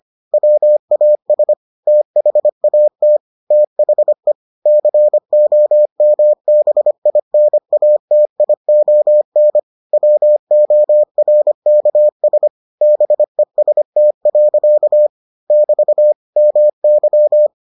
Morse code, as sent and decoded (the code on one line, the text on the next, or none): .-- .- ... / - .... .- - / - .... / -.-. --- -- -... .. -. .- - .. --- -. / .-- --- .-. -.- ... / -... ... - .-.-.- / -...- / -- -.--
WAS THAT THE COMBINATION WORKS BEST. <BT> MY